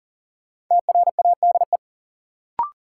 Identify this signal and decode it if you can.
Morse code, heard as a keyed tone, about 40 wpm, 700 Hz. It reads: TRADE
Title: Sentences from Top 600 Words - New Word Drill - 40wpm